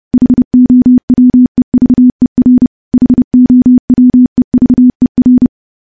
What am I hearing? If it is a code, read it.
Morse code, heard as a keyed tone, about 30 wpm, 250 Hz. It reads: HOWEVER HOWEVER
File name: morse-training-30WPM